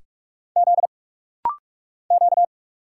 Morse code, heard as a keyed tone, about 46 wpm, 700 Hz. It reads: Z Q
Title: Single Letters - Rapid-Fire 45wpm